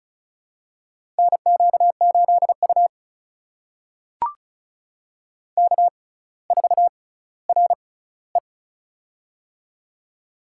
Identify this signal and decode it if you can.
Morse code, heard as a keyed tone, about 36 wpm, spaced as for 18 wpm, 700 Hz. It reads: NQ8U K4RE
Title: USA Call Signs - 4 Characters - 6x Character Spacing - 35wpm